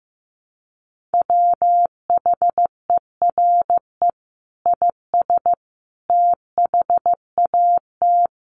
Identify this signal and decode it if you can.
Morse code, heard as a keyed tone, about 14 wpm, 700 Hz. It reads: WHERE IS THAT
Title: Sentences from Top 1500 Words 15wpm